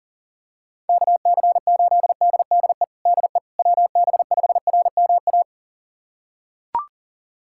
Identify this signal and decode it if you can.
Morse code, heard as a keyed tone, about 40 wpm, 690 Hz. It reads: KC8DDE DE WB5RMA
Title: USA Callsign DE Callsign - 6 Characters - 40wpm